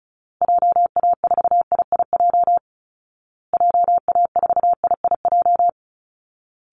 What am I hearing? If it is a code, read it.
Morse code, heard as a keyed tone, about 34 wpm, 700 Hz. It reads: JA4IIJ JA4IIJ
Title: Call Signs Repeated 2x - 35wpm